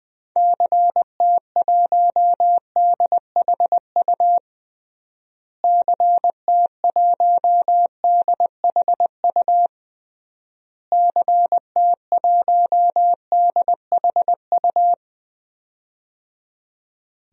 Morse code, heard as a keyed tone, about 20 wpm, 700 Hz. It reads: CT1DHU CT1DHU CT1DHU
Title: DXCC-100 Call Signs - 6 Characters - Repeated 3x - 20wpm